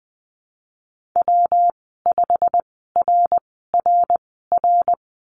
Morse code, heard as a keyed tone, about 20 wpm, spaced as for 15 wpm, 700 Hz. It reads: W5RRR
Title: 147 Lesson 24g - Letter y - Call Signs 20wpm